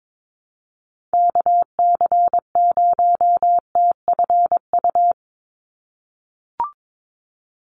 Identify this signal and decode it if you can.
Morse code, heard as a keyed tone, about 22 wpm, 700 Hz. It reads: KC0TFU
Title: USA Call Signs 22wpm